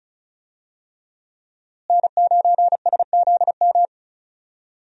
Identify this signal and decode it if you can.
Morse code, heard as a keyed tone, about 36 wpm, 700 Hz. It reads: N9SZM
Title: USA Call Signs - 5 Characters - Repeated 3x - 35wpm